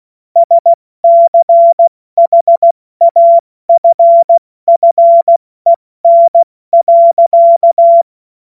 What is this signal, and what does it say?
Morse code, heard as a keyed tone, about 16 wpm, 680 Hz. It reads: SCHAFFEN.
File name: Morsekurs / Witze / Witz 21.November 2023 70 F80